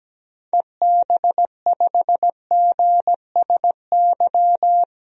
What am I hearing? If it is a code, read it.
Morse code, heard as a keyed tone, about 17 wpm, 700 Hz. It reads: EB5GSY
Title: DXCC-100 Call Signs - 6 Characters - 17wpm